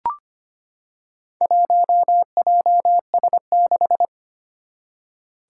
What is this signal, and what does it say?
Morse code, heard as a keyed tone, about 25 wpm, 700 Hz. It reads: 1JS6
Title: Four Challenging Letter-Number Combinations 25wpm